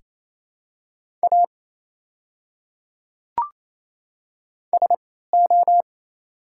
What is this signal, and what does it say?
Morse code, heard as a keyed tone, about 28 wpm, spaced as for 16 wpm, 700 Hz. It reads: A SO
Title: 3x Character Spacing - Top 200 Words - 28wpm